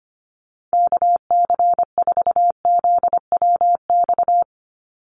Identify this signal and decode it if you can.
Morse code, heard as a keyed tone, about 25 wpm, 700 Hz. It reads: KC4ZWX
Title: USA Call Signs 25wpm